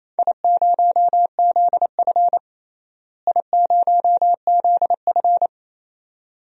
Morse code, heard as a keyed tone, about 28 wpm, 700 Hz. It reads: I0ZF I0ZF
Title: DXCC-100 Call Signs - 4 Characters - Repeated 2x - 28wpm